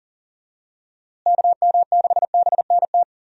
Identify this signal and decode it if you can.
Morse code, heard as a keyed tone, about 40 wpm, 700 Hz. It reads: KM6BNT
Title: USA Call Signs - 6 Characters - Repeated 2x - 40wpm